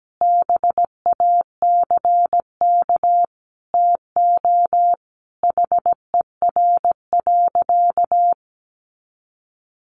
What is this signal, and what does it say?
Morse code, heard as a keyed tone, about 17 wpm, 700 Hz. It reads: BACK TO HER.
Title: Sentences from Top 300 Words 17wpm